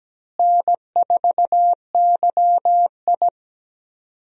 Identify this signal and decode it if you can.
Morse code, heard as a keyed tone, about 17 wpm, 700 Hz. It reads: N4YI
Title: USA Call Signs - 4 Characters - Repeated 2x - 17wpm